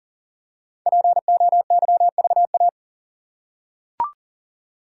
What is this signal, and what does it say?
Morse code, heard as a keyed tone, about 40 wpm, 700 Hz. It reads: POYVA
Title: Code Groups - 5 Letters - 40wpm